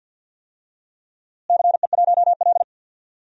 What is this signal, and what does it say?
Morse code, heard as a keyed tone, about 47 wpm, 700 Hz. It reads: CE1L
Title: DXCC-100 Call Signs - 4 Characters - Repeated 2x - 50wpm